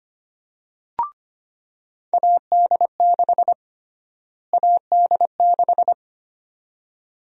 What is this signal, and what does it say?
Morse code, heard as a keyed tone, about 25 wpm, 700 Hz. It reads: AD6 AD6
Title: USA Call Sign Prefixes - Repeated 2x - 25wpm